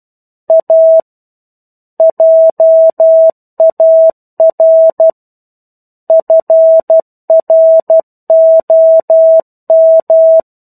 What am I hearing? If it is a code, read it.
Morse code, heard as a keyed tone, about 12 wpm, 650 Hz. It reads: A JAR FROM
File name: 01-Chapter-01 12 WPM